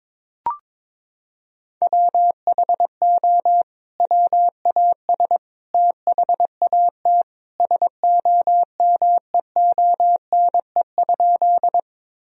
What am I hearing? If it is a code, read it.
Morse code, heard as a keyed tone, about 22 wpm, 700 Hz. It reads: WHO WAS THAT SOMEONE?
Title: Sentences from Top 1000 Words - Review Part 4 - 22wpm